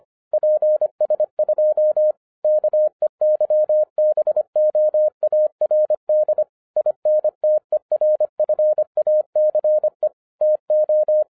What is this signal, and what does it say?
Morse code, heard as a keyed tone, about 25 wpm, 610 Hz. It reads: PS2 KEYBOARD INTERFACE TO
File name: 2011-10-mp3 / n09-CW TO PS2 CONVERTER 25wpm